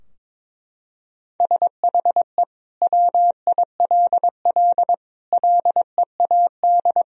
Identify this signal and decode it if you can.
Morse code, heard as a keyed tone, about 22 wpm, 700 Hz. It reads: SHE WILL LEAD
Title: Sentences from Top 1000 Words - Review Part 2 - 22wpm